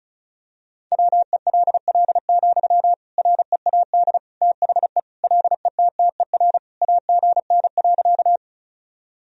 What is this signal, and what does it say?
Morse code, heard as a keyed tone, about 36 wpm, 700 Hz. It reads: WELL, READ THE LETTER AGN.
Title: Sentences from Top 300 Words and Abbreviations 35wpm